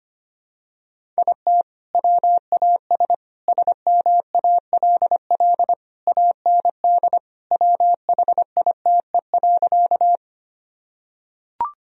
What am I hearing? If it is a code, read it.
Morse code, heard as a keyed tone, about 25 wpm, 700 Hz. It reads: IT WAS SMALL AND WHITE.
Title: Sentences from Top 1000 Words - Review Part 1 - 25wpm